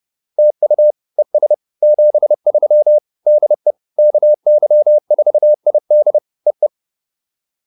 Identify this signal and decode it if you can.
Morse code, heard as a keyed tone, about 30 wpm, 600 Hz. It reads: TU ES 73 DE KY4ID E E
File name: POTA105 30